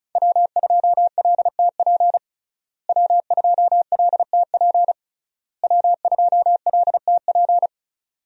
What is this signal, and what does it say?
Morse code, heard as a keyed tone, about 36 wpm, 700 Hz. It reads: W2LTP W2LTP W2LTP
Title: USA Call Signs - 5 Characters - Repeated 3x - 35wpm